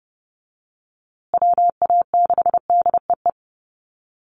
Morse code, heard as a keyed tone, about 29 wpm, 700 Hz. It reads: WA6DEE
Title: USA Call Signs Repeated 2x 30wpm